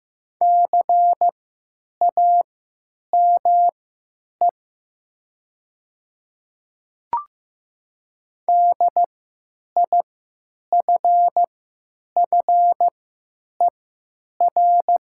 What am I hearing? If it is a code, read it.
Morse code, heard as a keyed tone, about 15 wpm, spaced as for 9 wpm, 700 Hz. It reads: CAME DIFFER
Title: 3x Character Spacing - Top 200 Words - 15wpm